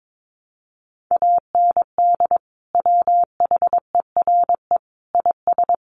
Morse code, heard as a keyed tone, about 22 wpm, 700 Hz. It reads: AND WHERE IS
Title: Sentences from Top 1300 Words 22wpm